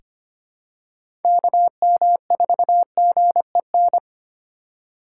Morse code, heard as a keyed tone, about 25 wpm, 700 Hz. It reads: KM4GEN
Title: USA Call Signs - 6 Characters - 2x Character Spacing - 25wpm